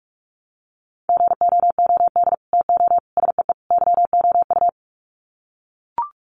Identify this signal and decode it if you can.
Morse code, heard as a keyed tone, about 44 wpm, 700 Hz. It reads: GOOD TO SEE YOU
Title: Sets of 4 Words 45wpm